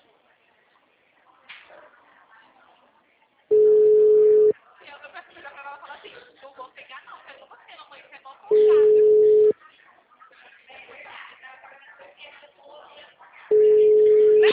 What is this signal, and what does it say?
Ringback tone (Europe, 425 Hz), with speech.
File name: out- -2015- - - .25382